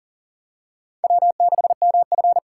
Morse code, heard as a keyed tone, about 40 wpm, 700 Hz. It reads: W6MF
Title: USA Call Signs - 6x Character Spacing - 40wpm